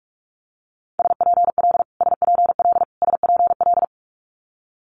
Morse code, heard as a keyed tone, about 46 wpm, 700 Hz. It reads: SPL SPL SPL
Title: Multi-letter Phonemes Repeated 3x 45wpm